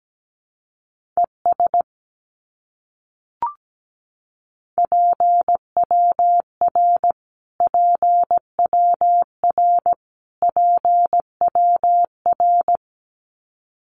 Morse code, heard as a keyed tone, about 17 wpm, 700 Hz. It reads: ES PWR PWR PWR
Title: Abbreviations Repeated 3x - 17wpm